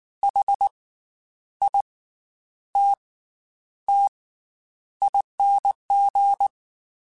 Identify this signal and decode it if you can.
Morse code, heard as a keyed tone, about 19 wpm, spaced as for 9 wpm, 800 Hz. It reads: HITTING
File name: Special INGs / Special INGs-mp3s / Slow ing @18